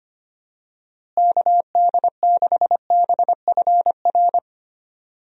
Morse code, heard as a keyed tone, about 25 wpm, 700 Hz. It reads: KD6BFR
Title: USA Call Signs - 6 Characters - Repeated 2x - 25wpm